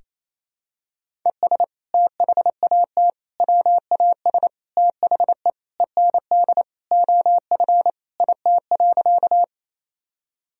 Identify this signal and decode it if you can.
Morse code, heard as a keyed tone, about 28 wpm, 700 Hz. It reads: ES THAT WAS THE END OF IT.
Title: Sentences from Top 200 Words and Abbreviations 28wpm